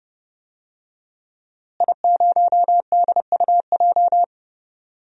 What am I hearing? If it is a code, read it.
Morse code, heard as a keyed tone, about 30 wpm, 700 Hz. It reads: I0DUJ
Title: DXCC-100 Call Signs - 5 Characters - Repeated 2x - 30wpm